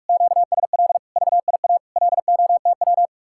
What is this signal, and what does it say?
Morse code, heard as a keyed tone, about 44 wpm, 700 Hz. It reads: QSL VIA LOTW